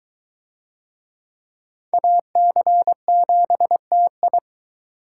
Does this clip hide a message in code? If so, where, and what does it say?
Yes: Morse code from 1.9 to 4.4 s, heard as a keyed tone, about 21 wpm, 700 Hz. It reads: AC7TI